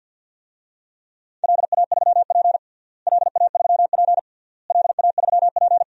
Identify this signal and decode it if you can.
Morse code, heard as a keyed tone, about 52 wpm, 700 Hz. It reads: LA3P LA3P LA3P
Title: DXCC-100 Call Signs - 4 Characters - Repeated 3x - 50wpm